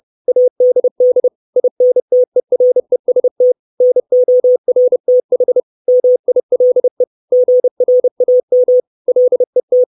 Morse code, heard as a keyed tone, about 30 wpm, 500 Hz. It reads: ADD INTEREST NORTH MILE GRAM LET
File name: 632 Words 30WPM